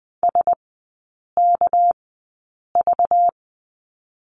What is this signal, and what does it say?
Morse code, heard as a keyed tone, about 20 wpm, spaced as for 8 wpm, 700 Hz. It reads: SKV
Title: 205 Lesson 33e - Letter j - Three Characters 20wpm